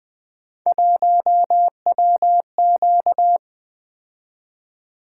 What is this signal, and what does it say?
Morse code, heard as a keyed tone, about 20 wpm, 700 Hz. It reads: 1WQ
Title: Three Letter-Number Combinations - 20wpm